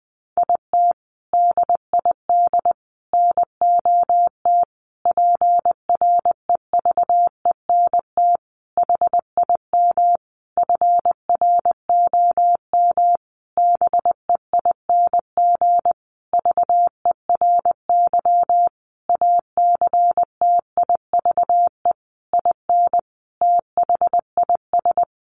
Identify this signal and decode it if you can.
Morse code, heard as a keyed tone, about 20 wpm, 700 Hz. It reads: IT DID NOT PREVENT HIM FROM BEING VERY ACTIVE IN THIS
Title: Sentences from Top 1500 Words 20wpm